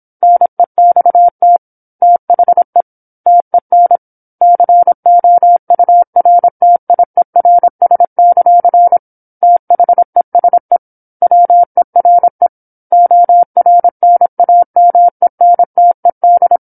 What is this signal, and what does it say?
Morse code, heard as a keyed tone, about 27 wpm, 700 Hz. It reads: NEXT THE TEN COURTIERS; THESE WERE ORNAMENTED
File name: 08-Chapter-08 26 WPM